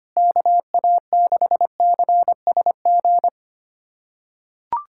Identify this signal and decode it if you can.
Morse code, heard as a keyed tone, about 25 wpm, 700 Hz. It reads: KA6CSG
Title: USA Callsign DE Callsign 25wpm